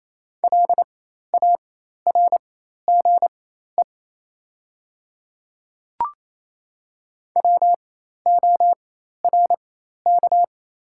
Morse code, heard as a keyed tone, about 27 wpm, spaced as for 13 wpm, 700 Hz. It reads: LARGE WORK
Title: 4x Character Spacing - Top 200 Words - 28wpm